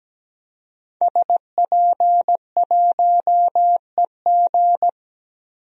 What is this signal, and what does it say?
Morse code, heard as a keyed tone, about 17 wpm, 700 Hz. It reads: SP1EG
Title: DXCC-100 Call Signs - 5 Characters - 17wpm